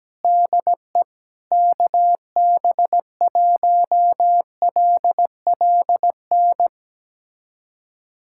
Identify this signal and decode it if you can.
Morse code, heard as a keyed tone, about 17 wpm, 700 Hz. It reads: DE KB1LLN
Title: USA Callsign DE Callsign - 6 Characters - 17wpm